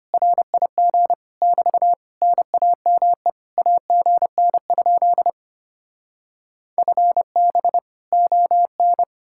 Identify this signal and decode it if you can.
Morse code, heard as a keyed tone, about 33 wpm, 700 Hz. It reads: RIG <BT> NAME AGN? FB ON